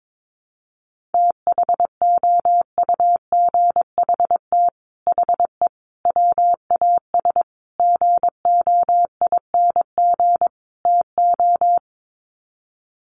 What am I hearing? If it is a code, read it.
Morse code, heard as a keyed tone, about 22 wpm, 700 Hz. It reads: THOUGHT HE WAS GOING TO
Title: Sets of 5 Words Encore2 22wpm